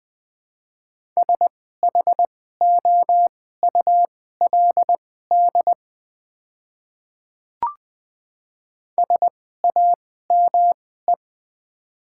Morse code, heard as a keyed tone, about 20 wpm, spaced as for 14 wpm, 700 Hz. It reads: SHOULD SAME
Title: 2x Character Spacing - Top 300 Words - 20wpm